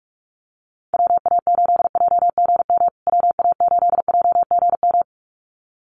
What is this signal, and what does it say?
Morse code, heard as a keyed tone, about 44 wpm, 700 Hz. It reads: WA8JGM WA8JGM
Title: USA Call Signs Repeated 2x 45wpm